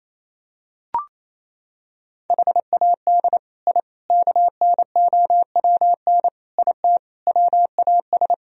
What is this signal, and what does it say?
Morse code, heard as a keyed tone, about 28 wpm, 700 Hz. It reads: HAD I KNOWN IT WAS
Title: Sentences from Top 1000 Words - Review Part 3 - 28wpm